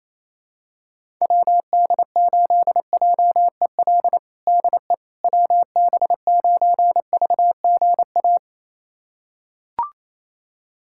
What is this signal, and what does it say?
Morse code, heard as a keyed tone, about 28 wpm, 700 Hz. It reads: WD8JEL DE WB9VGA
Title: USA Callsign DE Callsign - 6 Characters - 28wpm